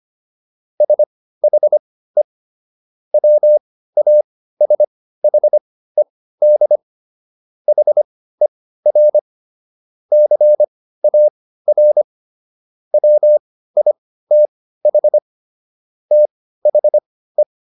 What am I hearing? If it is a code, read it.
Morse code, heard as a keyed tone, about 25 wpm, spaced as for 15 wpm, 600 Hz. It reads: SHE WASHED HER CAR WITH THE